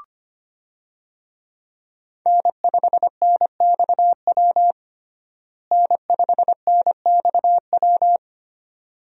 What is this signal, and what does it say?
Morse code, heard as a keyed tone, about 25 wpm, 700 Hz. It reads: N5NXW N5NXW
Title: USA Call Signs - 5 Characters - Repeated 2x - 25wpm